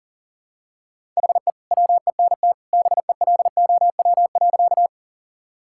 Morse code, heard as a keyed tone, about 40 wpm, 700 Hz. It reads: HE WENT BELOW.
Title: Sentences from Top 1000 Words - Review Part 5 - 40wpm